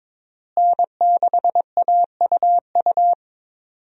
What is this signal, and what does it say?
Morse code, heard as a keyed tone, about 22 wpm, 700 Hz. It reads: N6AUU
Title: USA Call Signs - 5 Characters - Repeated 3x - 22wpm